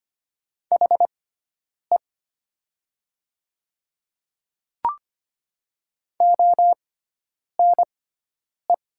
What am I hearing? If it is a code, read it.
Morse code, heard as a keyed tone, about 25 wpm, spaced as for 9 wpm, 700 Hz. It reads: HE ONE
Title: 6x Character Spacing - Top 100 Words - 25wpm